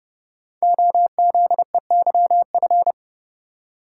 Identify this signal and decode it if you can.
Morse code, heard as a keyed tone, about 28 wpm, 700 Hz. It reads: OZEYF
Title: Code Groups - 5 Letter:Numbers - 30wpm